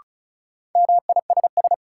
Morse code, heard as a keyed tone, about 34 wpm, 700 Hz. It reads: MISS
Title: Sentences from Top 500 Words - New Word Drill - 35wpm